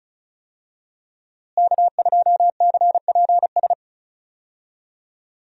Morse code, heard as a keyed tone, about 36 wpm, 700 Hz. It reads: K2CPS
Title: USA Call Signs - 5 Characters - Repeated 3x - 35wpm